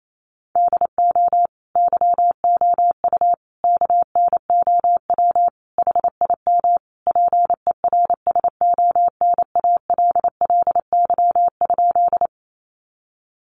Morse code, heard as a keyed tone, about 28 wpm, 700 Hz. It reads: DO YOU KNOW HIM PERSONALLY?
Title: Sentences from Top 1400 Words 28wpm